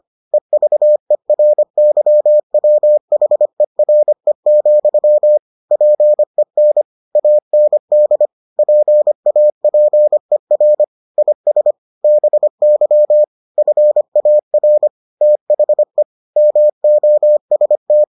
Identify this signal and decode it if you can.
Morse code, heard as a keyed tone, about 25 wpm, 600 Hz. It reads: EVERYWHERE, PEN AND PAPER IS BY FAR THE MOST